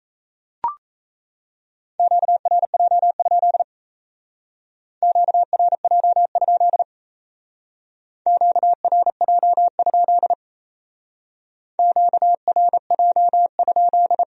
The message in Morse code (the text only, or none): QRJ? QRJ? QRJ? QRJ?